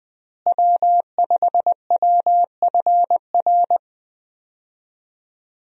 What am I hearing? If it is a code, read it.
Morse code, heard as a keyed tone, about 21 wpm, 700 Hz. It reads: W5WFR
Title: USA Call Signs - 5 Characters - Repeated 3x - 20wpm